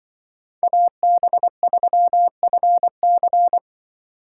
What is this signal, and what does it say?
Morse code, heard as a keyed tone, about 24 wpm, 700 Hz. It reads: AB3FC
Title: Speed Racing - USA 2x2 Call Signs 20wpm